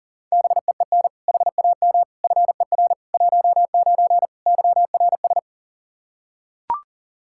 Morse code, heard as a keyed tone, about 37 wpm, 700 Hz. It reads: BEEN HAM FER 19 YRS
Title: Practice QSO - Exchange 3 - More Info - 40wpm